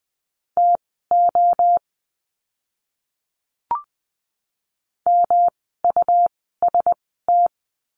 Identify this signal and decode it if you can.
Morse code, heard as a keyed tone, about 20 wpm, spaced as for 15 wpm, 700 Hz. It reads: TO MUST
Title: 108 Lesson 18f - Letter m - Words 20wpm